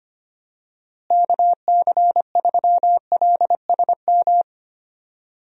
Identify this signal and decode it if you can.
Morse code, heard as a keyed tone, about 25 wpm, 700 Hz. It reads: KC3LSM
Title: USA Call Signs - 6 Characters - Repeated 2x - 25wpm